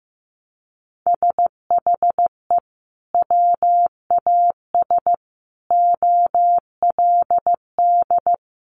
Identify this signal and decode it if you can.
Morse code, heard as a keyed tone, about 15 wpm, 700 Hz. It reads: SHE WAS OLD
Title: Sentences from Top 800 Words 15wpm